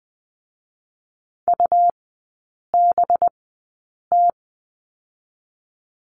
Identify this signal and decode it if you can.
Morse code, heard as a keyed tone, about 20 wpm, 700 Hz. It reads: U B T
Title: 218 Lesson 35e - Number 0 - Three Characters 20wpm